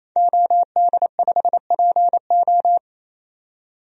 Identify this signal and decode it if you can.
Morse code, heard as a keyed tone, about 28 wpm, 700 Hz. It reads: OD5PO
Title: DXCC-100 Call Signs - 5 Characters - 28wpm